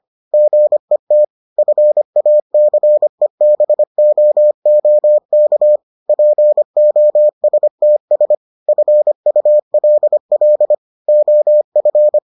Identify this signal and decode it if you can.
Morse code, heard as a keyed tone, about 25 wpm, 600 Hz. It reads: GET FACEBOOK POSTS FULL OF